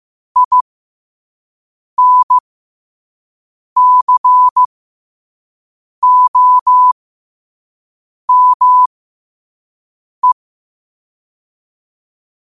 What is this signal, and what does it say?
Morse code, heard as a keyed tone, about 15 wpm, spaced as for 5 wpm, 1000 Hz. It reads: INCOME
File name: 5wpm